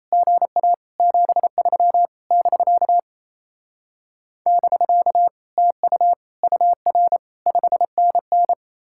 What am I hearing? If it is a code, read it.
Morse code, heard as a keyed tone, about 36 wpm, 700 Hz. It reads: GA 73 <BK> <BK> TU UR 5NN